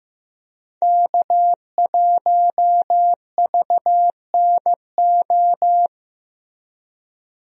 Morse code, heard as a keyed tone, about 15 wpm, 700 Hz. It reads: K1VNO
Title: USA Call Signs - 5 Characters - Repeated 2x - 15wpm